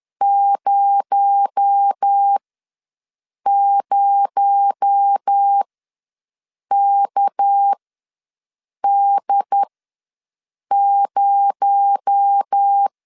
Morse code, heard as a keyed tone, about 10 wpm, spaced as for 6 wpm, 800 Hz. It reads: 00KD0
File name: LEZIONI / 6 PKX0 / 6 PKX0 ws 10wpm 4220